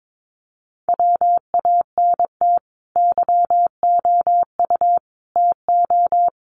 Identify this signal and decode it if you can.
Morse code, heard as a keyed tone, about 22 wpm, 700 Hz. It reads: WANT YOU TO
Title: Sets of 5 Words Encore 22wpm